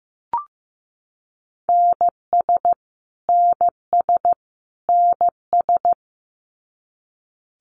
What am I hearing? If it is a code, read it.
Morse code, heard as a keyed tone, about 15 wpm, 700 Hz. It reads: NS NS NS
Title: Canadian Provinces and Territories - Repeated 3x - 15wpm